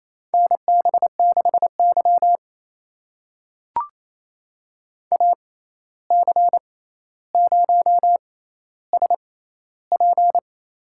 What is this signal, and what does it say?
Morse code, heard as a keyed tone, about 28 wpm, spaced as for 13 wpm, 700 Hz. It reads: NB6Y AC0SP
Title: USA Call Signs - 6x Character Spacing - 28wpm